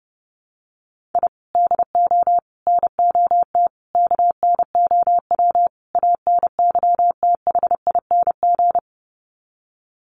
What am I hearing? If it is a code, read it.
Morse code, heard as a keyed tone, about 30 wpm, 700 Hz. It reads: I DO NOT KNOW ANYTHING
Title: Sets of 5 Words Encore 30wpm